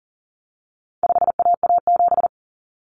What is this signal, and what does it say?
Morse code, heard as a keyed tone, about 40 wpm, 700 Hz. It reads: 5AA7
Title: Common Cut Numbers - 4-Digit 40wpm